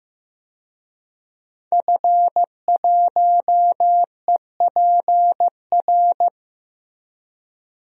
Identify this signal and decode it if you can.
Morse code, heard as a keyed tone, about 15 wpm, 700 Hz. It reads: F1EPR
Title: DXCC-100 Call Signs - 5 Characters - Repeated 3x - 15wpm